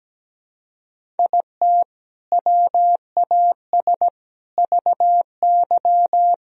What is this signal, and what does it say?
Morse code, heard as a keyed tone, about 17 wpm, 700 Hz. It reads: IT WAS VY
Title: Sentences from Top 500 Words and Abbreviations 17wpm